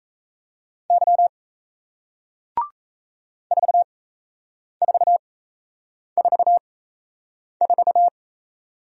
Morse code, heard as a keyed tone, about 38 wpm, spaced as for 8 wpm, 700 Hz. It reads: Y 4444